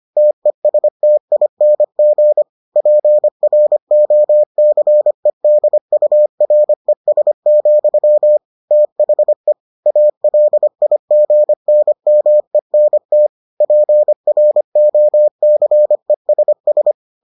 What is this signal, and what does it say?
Morse code, heard as a keyed tone, about 25 wpm, 600 Hz. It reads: TESTING PROCEDURES, THE ALIGNMENT PROCESS